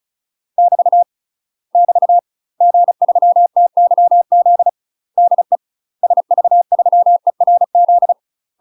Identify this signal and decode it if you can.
Morse code, heard as a keyed tone, about 35 wpm, 700 Hz. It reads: <BT> <BT> G3TYZ DE SV3ERZ
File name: UK MASTER QSOs TRAINING / 35WPM / UK MASTER QSOs 0026